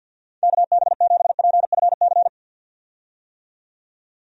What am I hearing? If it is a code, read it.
Morse code, heard as a keyed tone, about 46 wpm, 700 Hz. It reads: KB7PFC
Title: USA Call Signs - 6 Characters - 50wpm